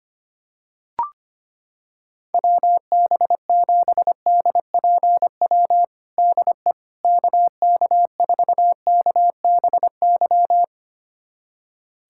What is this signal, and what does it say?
Morse code, heard as a keyed tone, about 25 wpm, 700 Hz. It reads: WB7DPW DE KK4KBY
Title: USA Callsign DE Callsign - 6 Characters - 25wpm